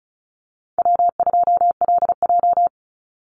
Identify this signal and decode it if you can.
Morse code, heard as a keyed tone, about 34 wpm, 700 Hz. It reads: W2LJ
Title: Easy Call Signs Repeated 3x - 35wpm